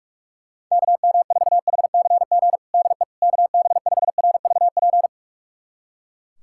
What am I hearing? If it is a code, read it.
Morse code, heard as a keyed tone, about 42 wpm, 700 Hz. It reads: KM4HCG DE KB5RVP
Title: USA Callsign DE Callsign 45wpm